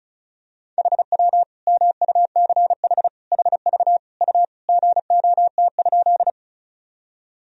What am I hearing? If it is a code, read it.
Morse code, heard as a keyed tone, about 36 wpm, 700 Hz. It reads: HW MUCH HV U GOT?
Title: Sentences from Top 300 Words and Abbreviations 35wpm